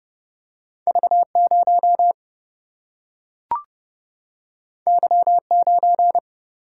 Morse code, heard as a keyed tone, about 30 wpm, 700 Hz. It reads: V0 Y9
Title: Single-Letter Single-Number 30wpm